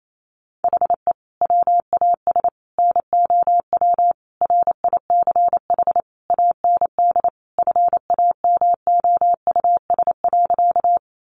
Morse code, heard as a keyed tone, about 28 wpm, 700 Hz. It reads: HE WAS NOW RICH AND FAMOUS.
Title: Sentences from Top 1500 Words 28wpm